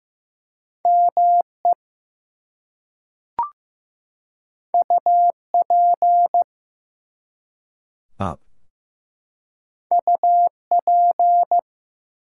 Morse code, heard as a keyed tone, about 15 wpm, 700 Hz. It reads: ME UP UP
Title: Two Letter Words - 15wpm